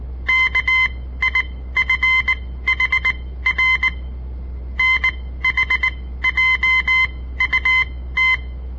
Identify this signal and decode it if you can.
Morse code, heard as a keyed tone, about 19 wpm, spaced as for 14 wpm, 2050 Hz. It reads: KIFHR NHJUT
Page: On the air Morse code practice from the Southern Montana Amatuer Radio Association; all code, no read back